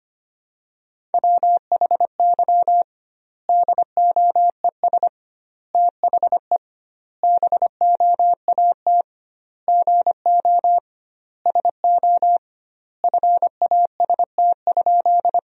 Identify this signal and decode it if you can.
Morse code, heard as a keyed tone, about 25 wpm, 700 Hz. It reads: WHY DOES THE BOAT GO SO FAST?
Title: 2x Word-Spacing - Sentences from Top 500 Words - 25wpm